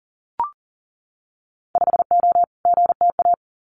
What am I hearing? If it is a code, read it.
Morse code, heard as a keyed tone, about 41 wpm, 700 Hz. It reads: 5O GTA
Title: Winter Field Day Exchanges Encore 40wpm